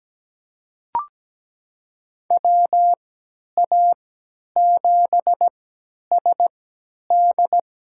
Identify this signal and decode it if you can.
Morse code, heard as a keyed tone, about 17 wpm, spaced as for 10 wpm, 700 Hz. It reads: WA7SD
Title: USA Call Signs - 6 Characters - 3x Character Spacing - 17wpm